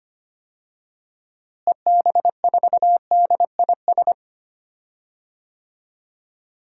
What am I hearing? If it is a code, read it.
Morse code, heard as a keyed tone, about 25 wpm, 700 Hz. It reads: EB4DIS
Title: DXCC-100 Call Signs - 6 Characters - Repeated 3x - 25wpm